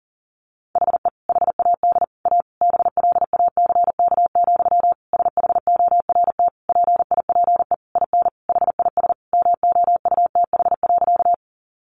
Morse code, heard as a keyed tone, about 40 wpm, 700 Hz. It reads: HE HAD A BLACK, SHORT PIPE IN HIS MOUTH.
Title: Sentences from Top 1500 Words 40wpm